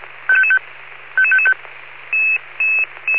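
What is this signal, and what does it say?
Five-tone selective call (ZVEI), heard as quick runs of five tones, tones of about 70 ms. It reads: 55555
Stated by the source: ZVEI five-tone selective call 11111 received on an SDR.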